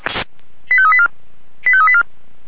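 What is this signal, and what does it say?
Five-tone selective call (ZVEI), heard as quick runs of five tones, tones of about 70 ms. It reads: 85384 85384
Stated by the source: ZVEI five-tone call received by SDR for the Orlamünde fire brigade (Feuerwehr Orlamünde).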